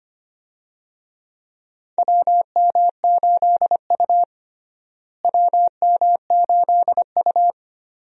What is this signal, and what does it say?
Morse code, heard as a keyed tone, about 25 wpm, 700 Hz. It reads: WM8U WM8U
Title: USA Call Signs - 4 Characters - Repeated 3x - 25wpm